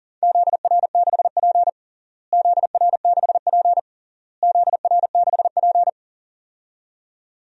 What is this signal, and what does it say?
Morse code, heard as a keyed tone, about 40 wpm, 700 Hz. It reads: ZR6P ZR6P ZR6P
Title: DXCC-100 Call Signs - 4 Characters - Repeated 3x - 40wpm